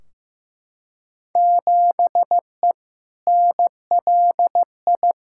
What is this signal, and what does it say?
Morse code, heard as a keyed tone, about 15 wpm, 700 Hz. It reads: 7E NLI
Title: Field Day Exchanges Encore - 15wpm